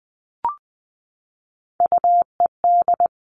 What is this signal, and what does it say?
Morse code, heard as a keyed tone, about 20 wpm, 700 Hz. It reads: UED
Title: Three Challenging Letter-Number Combinations 20wpm